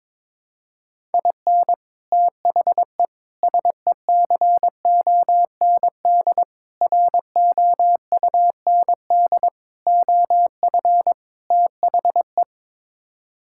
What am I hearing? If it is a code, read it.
Morse code, heard as a keyed tone, about 22 wpm, 700 Hz. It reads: IN THE SECOND ROUND OF THE
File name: practice / 22wpm / Sets of 6 Words - 22wpm